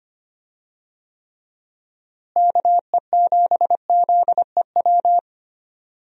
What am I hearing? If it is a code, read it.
Morse code, heard as a keyed tone, about 25 wpm, 700 Hz. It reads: KE7ZEW
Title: USA Call Signs - 6 Characters - Repeated 3x - 25wpm